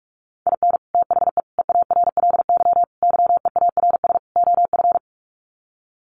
Morse code, heard as a keyed tone, about 44 wpm, 700 Hz. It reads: IN THE EARLY YEARS OF
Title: Sets of 5 Words Encore 45wpm